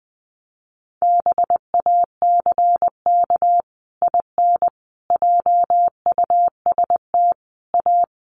Morse code, heard as a keyed tone, about 19 wpm, 700 Hz. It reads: BACK IN JUST A
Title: Sets of 5 Words Encore 20wpm